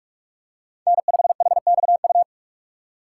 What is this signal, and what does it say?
Morse code, heard as a keyed tone, about 46 wpm, 700 Hz. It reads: N5HXU